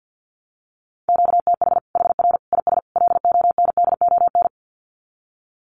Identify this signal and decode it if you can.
Morse code, heard as a keyed tone, about 48 wpm, 700 Hz. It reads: QTH HR IS LONDON
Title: Common QSO Phrases 50wpm